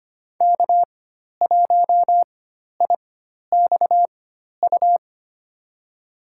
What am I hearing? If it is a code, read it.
Morse code, heard as a keyed tone, about 25 wpm, spaced as for 12 wpm, 700 Hz. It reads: K1IXU
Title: USA Call Signs - 5 Characters - 4x Character Spacing - 25wpm